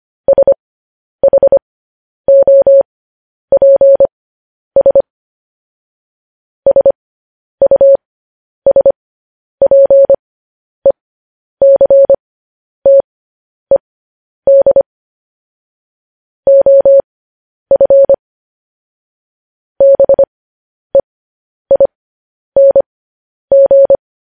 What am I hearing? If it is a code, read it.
Morse code, heard as a keyed tone, about 25 wpm, spaced as for 10 wpm, 550 Hz. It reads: SHOPS SUSPECTED OF BEING
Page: news headlines in morse code at 10 wpm